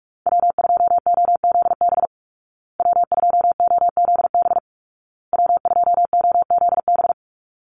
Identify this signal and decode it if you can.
Morse code, heard as a keyed tone, about 44 wpm, 700 Hz. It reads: W2OZB W2OZB W2OZB
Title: Call Signs Repeated 2x - 45wpm